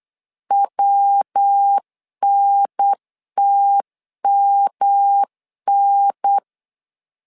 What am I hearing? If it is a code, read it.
Morse code, heard as a keyed tone, about 8 wpm, 800 Hz. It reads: WNTMN
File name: LEZIONI / 3 MWRO / 3 MW 8wpm 2501